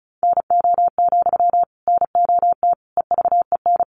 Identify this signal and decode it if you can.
Morse code, heard as a keyed tone, about 34 wpm, 700 Hz. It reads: NO, NOT EVEN
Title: Sentences from Top 1300 Words 35wpm